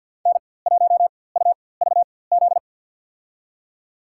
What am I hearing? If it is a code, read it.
Morse code, heard as a keyed tone, about 53 wpm, spaced as for 24 wpm, 700 Hz. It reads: N1UVZ
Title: USA Call Signs - 4x Character Spacing - 50wpm